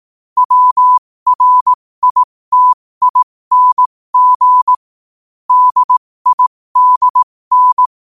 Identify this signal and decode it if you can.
Morse code, heard as a keyed tone, about 18 wpm, spaced as for 15 wpm, 1000 Hz. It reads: WRITING DIDN
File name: qotdfiles / 15wpm